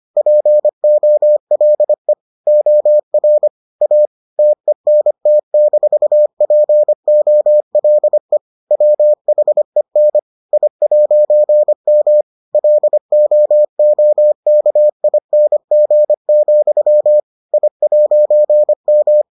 Morse code, heard as a keyed tone, about 25 wpm, 600 Hz. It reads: POLE OR A TENT-POLE WHEN I'M LOOKING, I'M